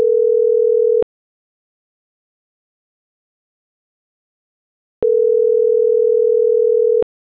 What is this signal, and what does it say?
Ringback tone (North America), 440 + 480 Hz.